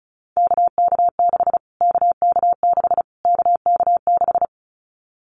Morse code, heard as a keyed tone, about 34 wpm, 700 Hz. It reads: KK6 KK6 KK6
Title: Callsign Prefixes for Top 25 DXCC - Repeated 3x - 35wpm